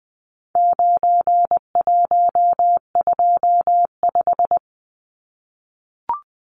Morse code, heard as a keyed tone, about 20 wpm, 700 Hz. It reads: 9125
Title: Numbers - 4-Digit 20wpm